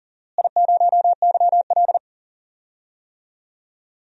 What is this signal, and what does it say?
Morse code, heard as a keyed tone, about 40 wpm, 700 Hz. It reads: I0YL
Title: DXCC-100 Call Signs - 4 Characters - Repeated 3x - 40wpm